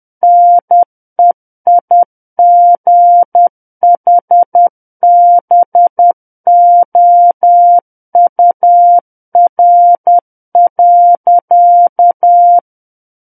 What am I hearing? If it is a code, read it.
Morse code, heard as a keyed tone, about 10 wpm, 700 Hz. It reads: NEIGHBOUR.
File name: Book One / 01-Chapter 1 - 10 WPM